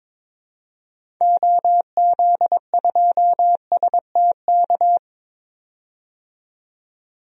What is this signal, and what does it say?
Morse code, heard as a keyed tone, about 21 wpm, 700 Hz. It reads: OZ2STK
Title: DXCC-100 Call Signs - 6 Characters - 22wpm